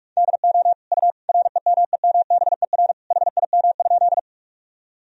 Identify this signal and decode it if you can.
Morse code, heard as a keyed tone, about 46 wpm, 700 Hz. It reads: DO U REMEMBER HIM?
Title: Sentences from Top 400 Words and Abbreviations 45wpm